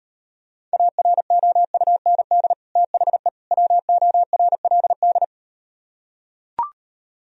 Morse code, heard as a keyed tone, about 38 wpm, 700 Hz. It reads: AROUND THE WORLD